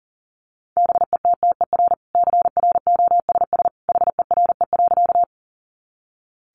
Morse code, heard as a keyed tone, about 40 wpm, 700 Hz. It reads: BETTER CROSS HERE.
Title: Sentences from Top 400 Words 40wpm